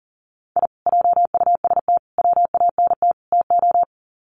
Morse code, heard as a keyed tone, about 40 wpm, 700 Hz. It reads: I JUST WANT TO
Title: Sets of 4 Words 40wpm